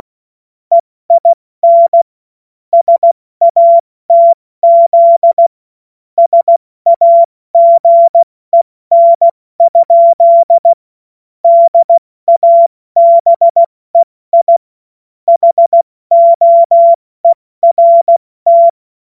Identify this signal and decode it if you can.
Morse code, heard as a keyed tone, about 16 wpm, 680 Hz. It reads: EIN SATZ SAGEN? DABEI HOERT
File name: Morsekurs / Witze / Witz 10. August 2021, 70 F80